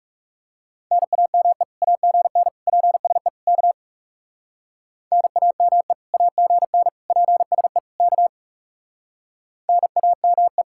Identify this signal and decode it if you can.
Morse code, heard as a keyed tone, about 39 wpm, 700 Hz. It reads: NAME AGN PSE K NAME AGN PSE K NAME